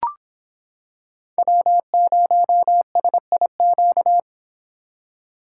Morse code, heard as a keyed tone, about 26 wpm, 700 Hz. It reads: W0SIQ